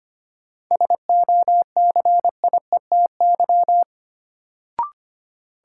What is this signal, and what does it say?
Morse code, heard as a keyed tone, about 25 wpm, 700 Hz. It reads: SOCIETY
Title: Sentences from Top 700 Words - New Word Drill - 25wpm